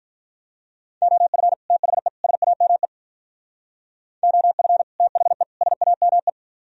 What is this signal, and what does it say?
Morse code, heard as a keyed tone, about 43 wpm, 700 Hz. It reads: OF THE SAME OF THE SAME
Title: Speed-Racing - Sets of 3 Words 35wpm